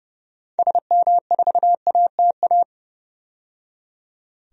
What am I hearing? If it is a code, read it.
Morse code, heard as a keyed tone, about 31 wpm, 700 Hz. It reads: SM4ATA
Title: DXCC-100 Call Signs - 6 Characters - 30wpm